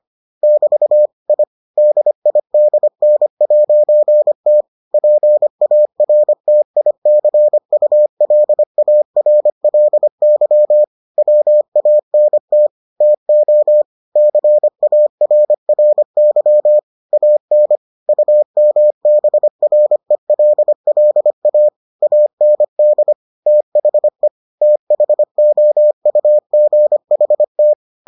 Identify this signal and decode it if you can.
Morse code, heard as a keyed tone, about 25 wpm, 600 Hz. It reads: <BT> I DIDN'T PARTICULARLY WANT TO CARRY AN UMBRELLA AND THE THOUGHT